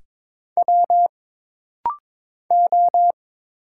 Morse code, heard as a keyed tone, about 22 wpm, 700 Hz. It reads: W O
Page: This rapid-fire timing is geared to push your ICR (Instant Character Recognition) skills.